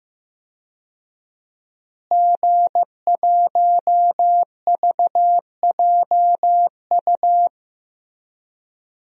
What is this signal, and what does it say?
Morse code, heard as a keyed tone, about 15 wpm, 700 Hz. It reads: G1VJU
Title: DXCC-100 Call Signs - 5 Characters - Repeated 3x - 15wpm